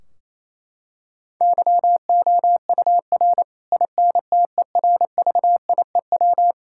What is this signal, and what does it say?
Morse code, heard as a keyed tone, about 28 wpm, 700 Hz. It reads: YOUR INTERVIEW
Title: Sentences from Top 1600 Words 28wpm